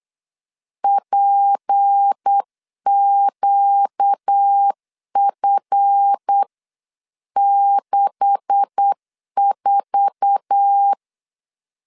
Morse code, heard as a keyed tone, about 8 wpm, 800 Hz. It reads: PQF64
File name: CW / DET / 1 9 1500C S A 8wpm 5637